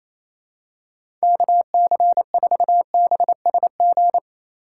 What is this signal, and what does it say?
Morse code, heard as a keyed tone, about 28 wpm, 700 Hz. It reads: KC4BSG
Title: USA Call Signs - 6 Characters - 3x Character Spacing - 28wpm